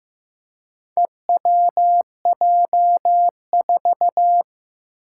Morse code, heard as a keyed tone, about 15 wpm, 700 Hz. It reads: EWJ4
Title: Four Challenging Letter-Number Combinations 15wpm